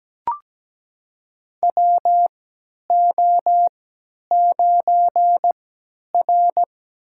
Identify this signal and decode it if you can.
Morse code, heard as a keyed tone, about 17 wpm, spaced as for 10 wpm, 700 Hz. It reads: WO9R
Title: USA Call Signs - 3x Character Spacing - 17wpm